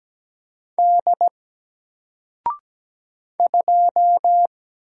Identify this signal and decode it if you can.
Morse code, heard as a keyed tone, about 17 wpm, 700 Hz. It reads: D 2
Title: Single Letter-Number Alt 17wpm